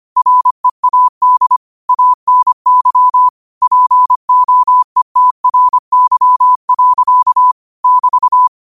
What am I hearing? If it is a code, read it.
Morse code, heard as a keyed tone, about 25 wpm, 1000 Hz. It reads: READ ANY POETRY. <BT>
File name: qotdfiles / 25wpm